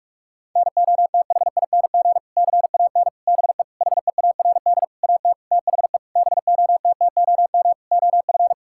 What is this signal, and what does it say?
Morse code, heard as a keyed tone, about 46 wpm, 700 Hz. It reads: NOTHING CAN BE HEARD AT THE BOTTOM OF
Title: Sentences from Top 1000 Words - Review Part 6 - 45wpm